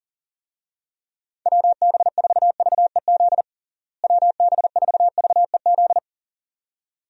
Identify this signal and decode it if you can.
Morse code, heard as a keyed tone, about 37 wpm, 700 Hz. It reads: WB4VEZ WB4VEZ
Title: USA Call Signs - 6 Characters - Repeated 2x - 40wpm